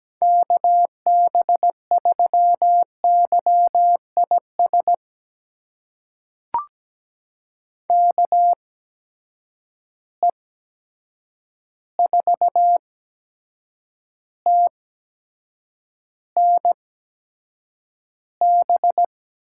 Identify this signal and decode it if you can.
Morse code, heard as a keyed tone, about 17 wpm, spaced as for 7 wpm, 700 Hz. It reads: KB3YIS KE4TNB
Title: USA Call Signs - 6 Characters - 8x Character Spacing - 17wpm